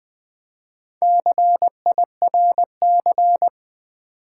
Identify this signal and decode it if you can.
Morse code, heard as a keyed tone, about 20 wpm, 700 Hz. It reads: CIRC
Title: Build A Word - 4 Letters At A Time - 12-Letter Words - 20wpm